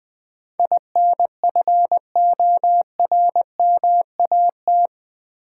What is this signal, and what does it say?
Morse code, heard as a keyed tone, about 20 wpm, 700 Hz. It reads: INFORMAT
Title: Build A Word - 4 Letters At A Time - 11-Letter Words - 20wpm